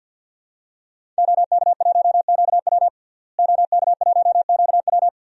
Morse code, heard as a keyed tone, about 53 wpm, 700 Hz. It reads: OK1QW OK1QW
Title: DXCC-100 Call Signs - 5 Characters - Repeated 2x - 50wpm